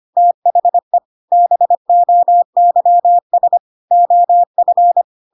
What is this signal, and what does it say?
Morse code, heard as a keyed tone, about 25 wpm, 700 Hz. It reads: THE BOYS OF